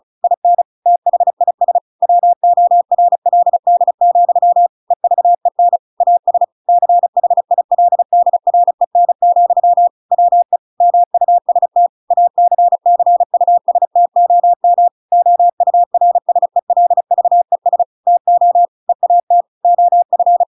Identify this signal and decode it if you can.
Morse code, heard as a keyed tone, about 34 wpm, 700 Hz. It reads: IN THIS WORLD, EVEN AS CHILDREN, WE MUST ACCUSTOM OURSELVES TO EAT OF
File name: Pinocchio / Pinocchio ch7 35 35 ik1hge